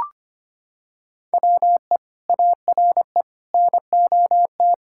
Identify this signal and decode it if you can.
Morse code, heard as a keyed tone, about 25 wpm, 700 Hz. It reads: WE ARE NOT
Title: Sentences from Top 1000 Words - Review Part 5 - 25wpm